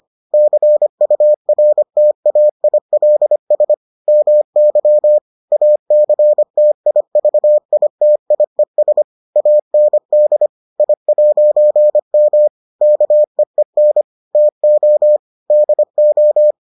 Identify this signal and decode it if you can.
Morse code, heard as a keyed tone, about 25 wpm, 600 Hz. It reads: CURTAILS MY ACTIVITIES AND I'M KEEN TO DO